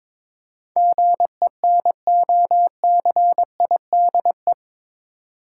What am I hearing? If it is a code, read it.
Morse code, heard as a keyed tone, about 22 wpm, 700 Hz. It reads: GENOCIDE
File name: practice / 22wpm / Eight Letter Words - 22wpm